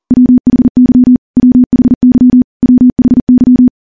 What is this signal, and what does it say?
Morse code, heard as a keyed tone, about 43 wpm, 250 Hz. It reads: WHY WHY WHY
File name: morse / morse-training-40WPM